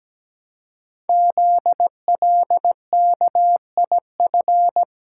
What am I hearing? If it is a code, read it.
Morse code, heard as a keyed tone, about 17 wpm, 700 Hz. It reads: ZLKIF
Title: Code Groups - 5 Letters - 17wpm